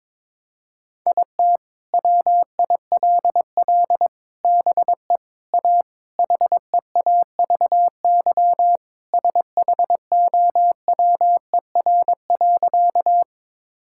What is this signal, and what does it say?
Morse code, heard as a keyed tone, about 22 wpm, 700 Hz. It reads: IT WILL BE A HEAVY SHOWER.
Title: Sentences from Top 1600 Words 22wpm